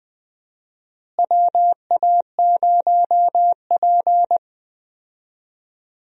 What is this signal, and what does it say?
Morse code, heard as a keyed tone, about 20 wpm, 700 Hz. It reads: WA0P
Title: USA Call Signs - 4 Characters - Repeated 2x - 20wpm